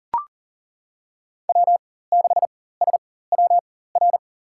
Morse code, heard as a keyed tone, about 38 wpm, spaced as for 18 wpm, 700 Hz. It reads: W6SWR
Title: USA Call Signs - 5 Characters - 4x Character Spacing - 40wpm